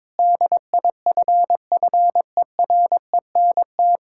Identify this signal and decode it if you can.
Morse code, heard as a keyed tone, about 22 wpm, 700 Hz. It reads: DIFFERENT
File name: Difficult Sentences with Hints - 22wpm